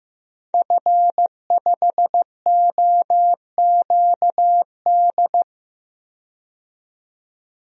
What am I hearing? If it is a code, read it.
Morse code, heard as a keyed tone, about 15 wpm, 700 Hz. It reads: F5OQD
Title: DXCC-100 Call Signs - 5 Characters - 15wpm